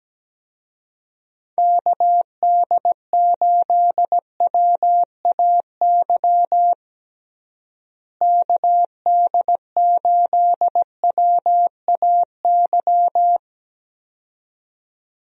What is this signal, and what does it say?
Morse code, heard as a keyed tone, about 17 wpm, 700 Hz. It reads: KD8WAY KD8WAY
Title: USA Call Signs - 6 Characters - Repeated 2x - 17wpm